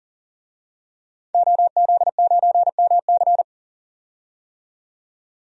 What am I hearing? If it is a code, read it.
Morse code, heard as a keyed tone, about 37 wpm, 700 Hz. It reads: OZ9MC
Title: DXCC-100 Call Signs - 5 Characters - 40wpm